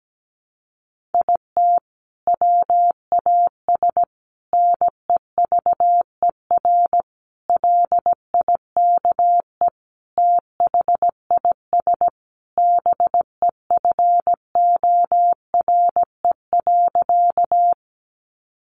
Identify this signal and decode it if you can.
Morse code, heard as a keyed tone, about 17 wpm, 700 Hz. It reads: IT WAS NEVER LIKE THIS BEFORE.
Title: A Sound Of Thunder 17wpm